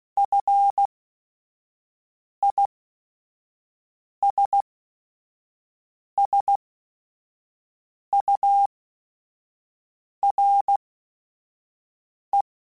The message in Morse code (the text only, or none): FISSURE